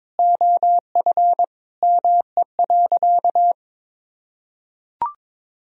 Morse code, heard as a keyed tone, about 22 wpm, 700 Hz. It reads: OF ME.
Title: Sentences from Top 1000 Words - Review Part 1 - 22wpm